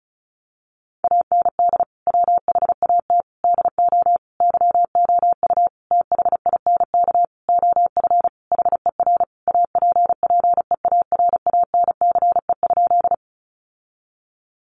Morse code, heard as a keyed tone, about 34 wpm, 700 Hz. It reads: AND WHAT DO YOU THINK OF HER APPEARANCE?
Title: Sentences from Top 1100 Words 35wpm